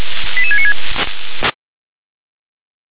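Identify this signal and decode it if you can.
Five-tone selective call (ZVEI), heard as quick runs of five tones, tones of about 70 ms. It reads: 88696
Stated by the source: ZVEI five-tone selective call received by SDR.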